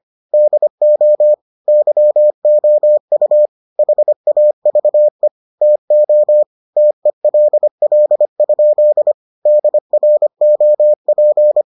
Morse code, heard as a keyed tone, about 25 wpm, 600 Hz. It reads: DO YOU HAVE TO TELL? DROP